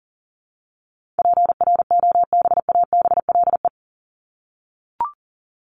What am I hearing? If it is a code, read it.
Morse code, heard as a keyed tone, about 37 wpm, 700 Hz. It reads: PROBABLE
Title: LY Suffix - Top 100 Words with Base Words 40wpm